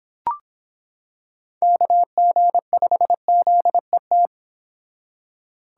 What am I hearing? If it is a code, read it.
Morse code, heard as a keyed tone, about 25 wpm, 700 Hz. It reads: KG5ZET